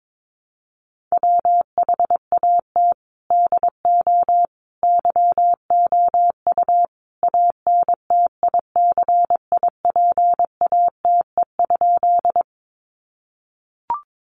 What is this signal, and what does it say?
Morse code, heard as a keyed tone, about 22 wpm, 700 Hz. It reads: WHAT DO YOU ANTICIPATE?
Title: Sentences from Top 1500 Words 22wpm